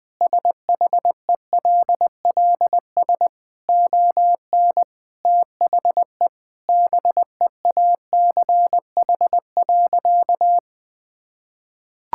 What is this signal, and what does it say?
Morse code, heard as a keyed tone, about 21 wpm, 700 Hz. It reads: SHELLS ON THE BEACH.
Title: Easy Sentences with Hints - 20wpm